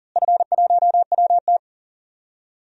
Morse code, heard as a keyed tone, about 40 wpm, 700 Hz. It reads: F1WT
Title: DXCC-100 Call Signs - 4 Characters - Repeated 3x - 40wpm